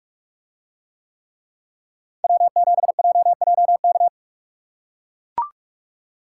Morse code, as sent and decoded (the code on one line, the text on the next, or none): .-- --... .--- .--- -.-
W7JJK